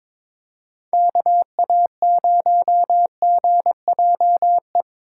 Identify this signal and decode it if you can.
Morse code, heard as a keyed tone, about 24 wpm, 700 Hz. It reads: KA0GJE
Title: Speed Racing - USA Call Signs 22wpm